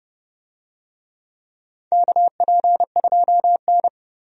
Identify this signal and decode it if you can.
Morse code, heard as a keyed tone, about 30 wpm, 700 Hz. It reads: KP2N
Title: DXCC-100 Call Signs - 4 Characters - 30wpm